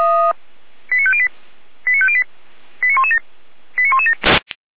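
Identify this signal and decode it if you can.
Five-tone selective call (ZVEI), heard as quick runs of five tones, tones of about 70 ms. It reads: 89508 89508 89117 89117
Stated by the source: ZVEI five-tone selective call 89080 received via SDR.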